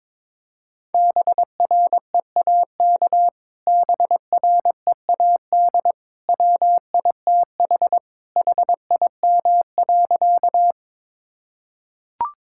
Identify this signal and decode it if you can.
Morse code, heard as a keyed tone, about 21 wpm, 700 Hz. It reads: BREAK BREAD WITH HIM.
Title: Sentences from Top 1600 Words 22wpm